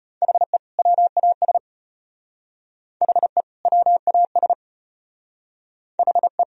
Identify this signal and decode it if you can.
Morse code, heard as a keyed tone, about 32 wpm, 700 Hz. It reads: HE WAS HE WAS HE